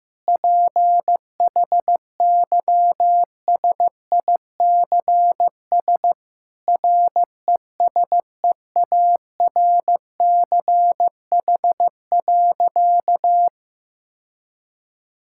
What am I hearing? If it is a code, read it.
Morse code, heard as a keyed tone, about 14 wpm, 700 Hz. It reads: PHYSICS RESEARCH.
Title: Sentences from Top 1000 Words - Review Part 5 - 15wpm